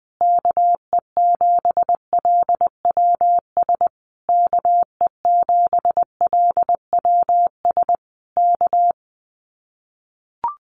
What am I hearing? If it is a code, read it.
Morse code, heard as a keyed tone, about 20 wpm, 700 Hz. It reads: KE7LWS KE7LWS K
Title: Common QSO Phrases Encore 20wpm